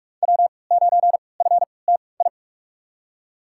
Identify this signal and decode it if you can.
Morse code, heard as a keyed tone, about 45 wpm, spaced as for 25 wpm, 700 Hz. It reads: W9FTI
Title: USA Call Signs - 3x Character Spacing - 45wpm